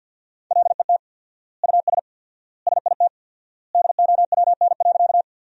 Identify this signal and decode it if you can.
Morse code, heard as a keyed tone, about 51 wpm, 700 Hz. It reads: LET US SIT DOWN.
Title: 4x Word-Spacing - Sentences from Top 500 Words - 50wpm